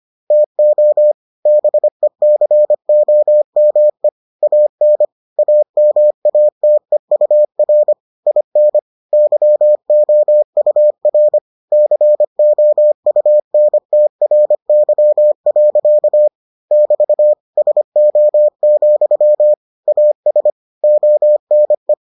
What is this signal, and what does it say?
Morse code, heard as a keyed tone, about 25 wpm, 600 Hz. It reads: TO BECOME AN AMATEUR IN YOUR COUNTRY. <BT> SO, AS ONE